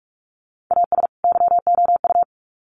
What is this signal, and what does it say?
Morse code, heard as a keyed tone, about 44 wpm, 700 Hz. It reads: AS YOU
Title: Sets of 2 Words 45wpm